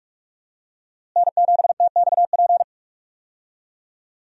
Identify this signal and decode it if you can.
Morse code, heard as a keyed tone, about 44 wpm, 700 Hz. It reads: N7TXP
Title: USA Call Signs - 5 Characters - Repeated 2x - 45wpm